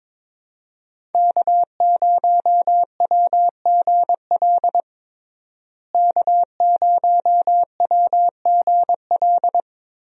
Morse code, heard as a keyed tone, about 22 wpm, 700 Hz. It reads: K0WGL K0WGL
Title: USA Call Signs - 5 Characters - Repeated 3x - 22wpm